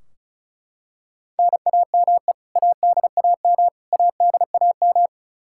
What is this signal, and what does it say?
Morse code, heard as a keyed tone, about 34 wpm, 700 Hz. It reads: NAME ADAM ADAM
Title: K1USN SST QSO Practice 35wpm